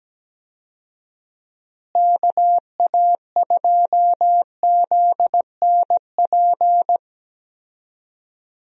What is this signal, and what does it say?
Morse code, heard as a keyed tone, about 17 wpm, 700 Hz. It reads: KA2ZNP
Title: USA Call Signs - 6 Characters - Repeated 2x - 17wpm